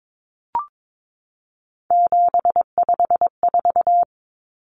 Morse code, heard as a keyed tone, about 22 wpm, 700 Hz. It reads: 754
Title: Common Cut Numbers - 3-Digit 22wpm